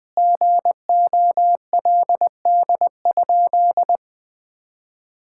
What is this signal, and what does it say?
Morse code, heard as a keyed tone, about 20 wpm, 700 Hz. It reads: GOLD?
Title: Sentences from Top 1000 Words - Review Part 6 - 20wpm